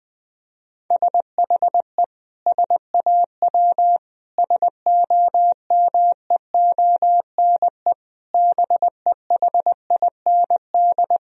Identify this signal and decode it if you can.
Morse code, heard as a keyed tone, about 20 wpm, 700 Hz. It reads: SHE SAW SOMEONE BEHIND
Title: Sentences from Top 1000 Words - Review Part 4 - 20wpm